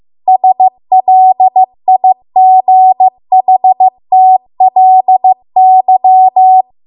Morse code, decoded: SLIGHTLY